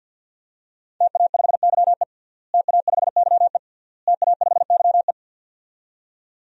Morse code, heard as a keyed tone, about 50 wpm, 700 Hz. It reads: TA5YE TA5YE TA5YE
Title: DXCC-100 Call Signs - 5 Characters - Repeated 3x - 50wpm